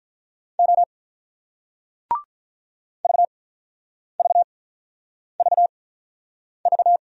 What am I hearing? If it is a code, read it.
Morse code, heard as a keyed tone, about 46 wpm, spaced as for 9 wpm, 700 Hz. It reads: O VVVV